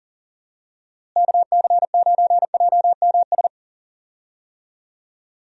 Morse code, heard as a keyed tone, about 40 wpm, 700 Hz. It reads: KC9JMS
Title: USA Call Signs - 6 Characters - Repeated 2x - 40wpm